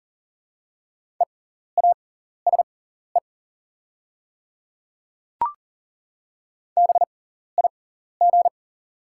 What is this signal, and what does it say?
Morse code, heard as a keyed tone, about 40 wpm, spaced as for 14 wpm, 700 Hz. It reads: EASE BIG
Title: 6x Character Spacing - Top 300 Words - 40wpm